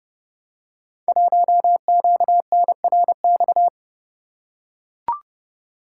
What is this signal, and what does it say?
Morse code, heard as a keyed tone, about 30 wpm, 700 Hz. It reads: 1QNRX
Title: Code Groups - 5 Letter:Numbers - 30wpm